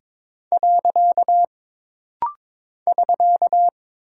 Morse code, heard as a keyed tone, about 22 wpm, 700 Hz. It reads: . <SK>
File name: practice / 22wpm / Single Punctuation-Prosign - Rapid-Fire - 22wpm